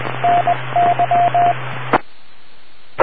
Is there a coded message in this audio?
Yes: Morse code from 0.2 to 1.5 s, heard as a keyed tone, about 20 wpm, 700 Hz, with radio noise. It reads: NY